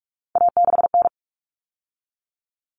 Morse code, heard as a keyed tone, about 44 wpm, 700 Hz. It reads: A6N
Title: All Cut Numbers - 3-Digit 45wpm